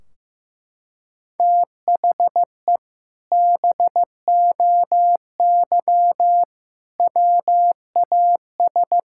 Morse code, heard as a keyed tone, about 15 wpm, 700 Hz. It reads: THE BOY WAS
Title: Sentences from Top 1600 Words 15wpm